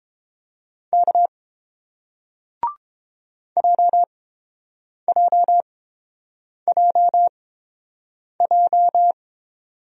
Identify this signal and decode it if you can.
Morse code, heard as a keyed tone, about 30 wpm, spaced as for 7 wpm, 700 Hz. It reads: K JJJJ